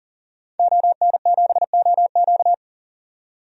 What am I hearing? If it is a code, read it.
Morse code, heard as a keyed tone, about 40 wpm, 700 Hz. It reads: ON7OQ
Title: DXCC-100 Call Signs - 5 Characters - 40wpm